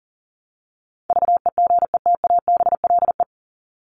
Morse code, heard as a keyed tone, about 40 wpm, 700 Hz. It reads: VEGETABLE
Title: Top 1251-1500 Words 40wpm